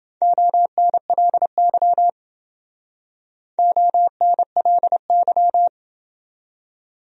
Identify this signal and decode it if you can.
Morse code, heard as a keyed tone, about 28 wpm, 700 Hz. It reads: ONLY ONLY